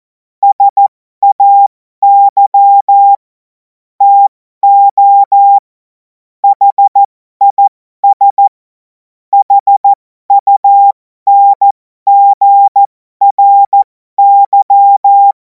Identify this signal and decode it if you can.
Morse code, heard as a keyed tone, about 14 wpm, spaced as for 12 wpm, 800 Hz. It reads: SAY TO HIS HUNGRY
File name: Morsekurs / Witze